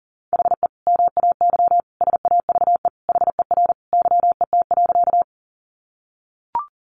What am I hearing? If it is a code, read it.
Morse code, heard as a keyed tone, about 40 wpm, 700 Hz. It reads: HE MAY SAVE HER YET.
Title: Sentences from Top 1100 Words 40wpm